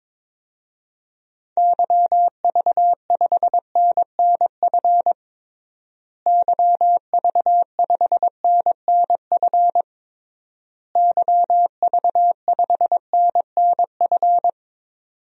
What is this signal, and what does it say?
Morse code, heard as a keyed tone, about 22 wpm, 700 Hz. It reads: YV5NNF YV5NNF YV5NNF
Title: DXCC-100 Call Signs - 6 Characters - Repeated 3x - 22wpm